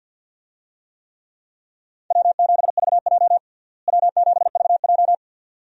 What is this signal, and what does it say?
Morse code, heard as a keyed tone, about 52 wpm, 700 Hz. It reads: W7VJ W7VJ
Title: USA Call Signs - 4 Characters - Repeated 2x - 50wpm